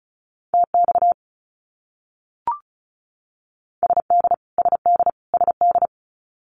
Morse code, heard as a keyed tone, about 34 wpm, 700 Hz. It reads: TX SD SD SD
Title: US States Repeated 3x 35wpm